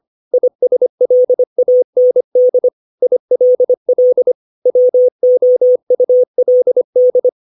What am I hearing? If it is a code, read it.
Morse code, heard as a keyed tone, about 25 wpm, 500 Hz. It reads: ISLAND ILL WOULD